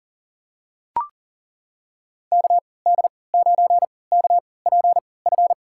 Morse code, heard as a keyed tone, about 40 wpm, spaced as for 23 wpm, 700 Hz. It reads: KD9KPF
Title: USA Call Signs - 6 Characters - 3x Character Spacing - 40wpm